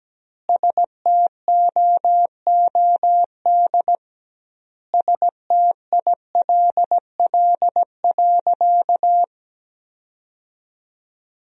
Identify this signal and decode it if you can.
Morse code, heard as a keyed tone, about 17 wpm, 700 Hz. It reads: STOOD STILL.
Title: 2x Word-Spacing - Sentences from Top 500 Words - 17wpm